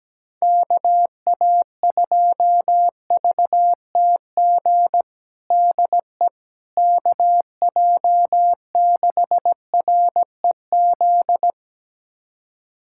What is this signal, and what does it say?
Morse code, heard as a keyed tone, about 17 wpm, 700 Hz. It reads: KA2VTG DE KJ6REZ
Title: USA Callsign DE Callsign - 6 Characters - 17wpm